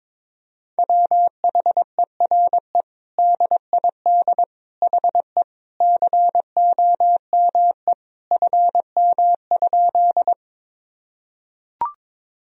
Morse code, heard as a keyed tone, about 22 wpm, 700 Hz. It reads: WHERE DID HE COME FM?
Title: Sentences from Top 200 Words and Abbreviations 22wpm